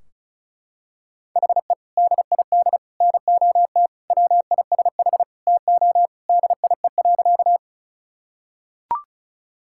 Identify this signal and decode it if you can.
Morse code, heard as a keyed tone, about 34 wpm, 700 Hz. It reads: HE DID NOT WISH TO DIE.
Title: Sentences from Top 1600 Words 35wpm